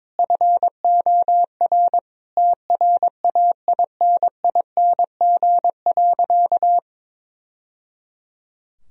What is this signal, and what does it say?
Morse code, heard as a keyed tone, about 22 wpm, 700 Hz. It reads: FOR TRAINING.
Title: Sentences from Top 1000 Words - Review Part 4 - 22wpm